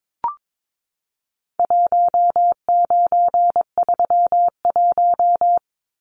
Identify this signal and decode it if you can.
Morse code, heard as a keyed tone, about 22 wpm, 700 Hz. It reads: 1931
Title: Numbers - 4-Digit 22wpm